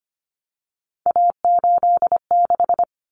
Morse code, heard as a keyed tone, about 25 wpm, 700 Hz. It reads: A86
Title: Common Cut Numbers - 3-Digit 25wpm